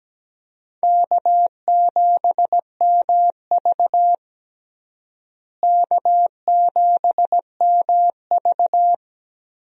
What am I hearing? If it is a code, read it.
Morse code, heard as a keyed tone, about 17 wpm, 700 Hz. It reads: K7MV K7MV
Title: USA Call Signs - 4 Characters - Repeated 2x - 17wpm